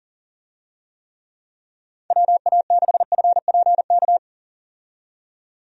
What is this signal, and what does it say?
Morse code, heard as a keyed tone, about 40 wpm, 690 Hz. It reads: WA6FPK
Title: USA Call Signs - 6 Characters - 40wpm